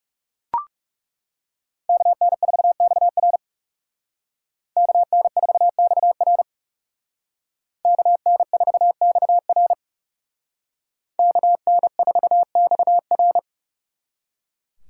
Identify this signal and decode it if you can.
Morse code, heard as a keyed tone, about 36 wpm, 700 Hz. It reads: KN4XR KN4XR KN4XR KN4XR
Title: Speed Racing - USA 2x2 Call Signs 30wpm